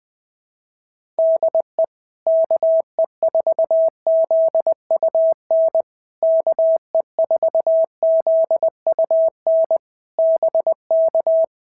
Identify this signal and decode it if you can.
Morse code, heard as a keyed tone, about 19 wpm, 650 Hz. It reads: DE KE4ZUN KE4ZUN BK
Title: POTA QSO Practice - 20wpm